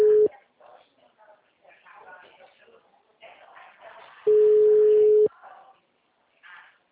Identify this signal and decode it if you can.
Ringback tone (Europe, 425 Hz), with speech.